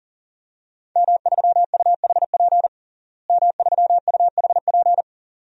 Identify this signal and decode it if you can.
Morse code, heard as a keyed tone, about 40 wpm, 700 Hz. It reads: M3UHP M3UHP
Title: DXCC-100 Call Signs - 5 Characters - Repeated 2x - 40wpm